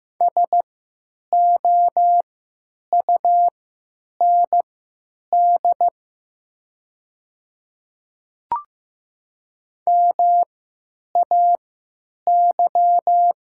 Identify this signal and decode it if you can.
Morse code, heard as a keyed tone, about 15 wpm, spaced as for 9 wpm, 700 Hz. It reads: SOUND MAY
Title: 3x Character Spacing - Top 100 Words - 15wpm